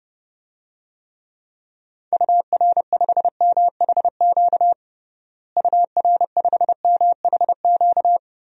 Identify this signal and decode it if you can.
Morse code, heard as a keyed tone, about 30 wpm, 700 Hz. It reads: UR5MHQ UR5MHQ
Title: DXCC-100 Call Signs - 6 Characters - Repeated 3x - 30wpm